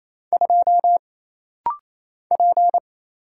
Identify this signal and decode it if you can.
Morse code, heard as a keyed tone, about 27 wpm, 700 Hz. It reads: 2 P
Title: Single Letter-Number Alt - Rapid-Fire 28wpm